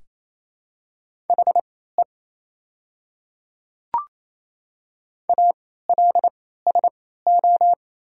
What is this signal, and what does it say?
Morse code, heard as a keyed tone, about 27 wpm, spaced as for 16 wpm, 700 Hz. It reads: HE ALSO
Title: 3x Character Spacing - Top 200 Words - 28wpm